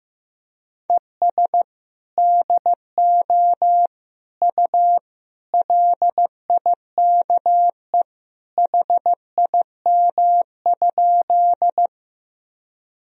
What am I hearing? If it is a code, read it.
Morse code, heard as a keyed tone, about 15 wpm, 700 Hz. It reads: ES DO U LIKE HIM?
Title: Sentences from Top 100 Words and Abbreviations 15wpm